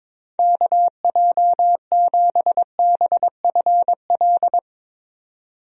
Morse code, heard as a keyed tone, about 21 wpm, 700 Hz. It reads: KJ7BFL
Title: USA Call Signs - 6 Characters - 22wpm